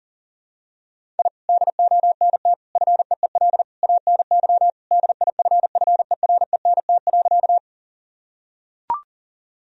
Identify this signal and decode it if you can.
Morse code, heard as a keyed tone, about 37 wpm, 700 Hz. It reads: I DONT FEEL ANY DIFFERENT.
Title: Sentences from Top 1000 Words - Review Part 3 - 40wpm